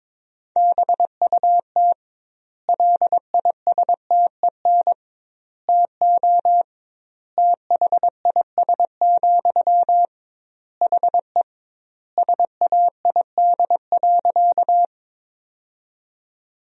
Morse code, heard as a keyed tone, about 22 wpm, 700 Hz. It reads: BUT LISTEN TO THIS, HE SAID.
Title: 2x Word-Spacing - Sentences from Top 400 Words - 22wpm